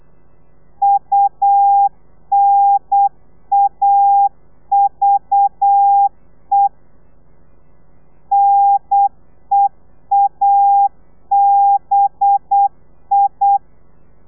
Morse code, heard as a keyed tone, about 8 wpm, 800 Hz. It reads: UNAVE NEABI
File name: telegrafia / i2tfj / Lez4 35